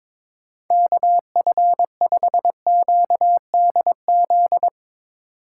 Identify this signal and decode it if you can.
Morse code, heard as a keyed tone, about 22 wpm, 700 Hz. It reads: KF5QDZ
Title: USA Call Signs - 6 Characters - 6x Character Spacing - 22wpm